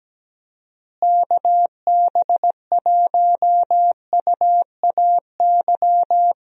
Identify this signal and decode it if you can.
Morse code, heard as a keyed tone, about 17 wpm, 700 Hz. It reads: KB1UAY
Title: USA Call Signs - 6 Characters - Repeated 2x - 17wpm